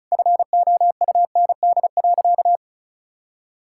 Morse code, heard as a keyed tone, about 36 wpm, 700 Hz. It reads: FOUND.
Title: Sentences from Top 1600 Words 35wpm